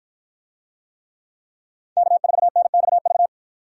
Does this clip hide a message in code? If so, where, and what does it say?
Yes: Morse code from 2.0 to 3.3 s, heard as a keyed tone, about 53 wpm, 700 Hz. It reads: K4NXV